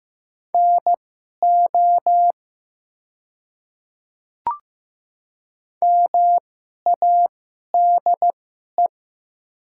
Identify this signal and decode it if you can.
Morse code, heard as a keyed tone, about 15 wpm, spaced as for 11 wpm, 700 Hz. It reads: NO MADE
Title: 2x Character Spacing - Top 300 Words - 15wpm